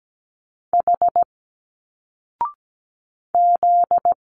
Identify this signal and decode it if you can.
Morse code, heard as a keyed tone, about 17 wpm, 700 Hz. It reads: H Z
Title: Single Letters 17wpm